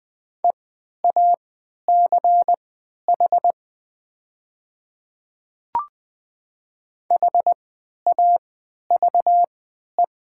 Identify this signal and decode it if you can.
Morse code, heard as a keyed tone, about 20 wpm, spaced as for 11 wpm, 700 Hz. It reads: EACH HAVE
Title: 3x Character Spacing - Top 100 Words - 20wpm